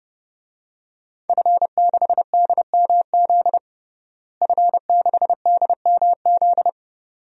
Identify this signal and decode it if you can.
Morse code, heard as a keyed tone, about 32 wpm, 700 Hz. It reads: F6DMZ F6DMZ
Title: DXCC-100 Call Signs - 5 Characters - Repeated 2x - 30wpm